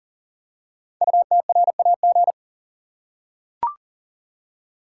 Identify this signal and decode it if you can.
Morse code, heard as a keyed tone, about 39 wpm, 700 Hz. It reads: UTRAG
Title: Code Groups - 5 Letter:Numbers - 40wpm